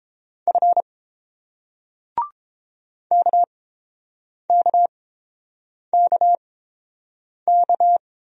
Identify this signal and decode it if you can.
Morse code, heard as a keyed tone, about 30 wpm, spaced as for 7 wpm, 700 Hz. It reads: F KKKK